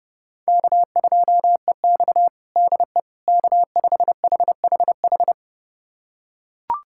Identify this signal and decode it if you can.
Morse code, heard as a keyed tone, about 30 wpm, 700 Hz. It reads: K2EX DE K5HHH
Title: USA Callsign DE Callsign 30wpm